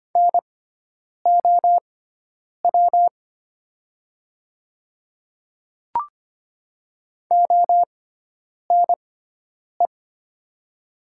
Morse code, heard as a keyed tone, about 25 wpm, spaced as for 9 wpm, 700 Hz. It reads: NOW ONE
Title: 6x Character Spacing - Top 100 Words - 25wpm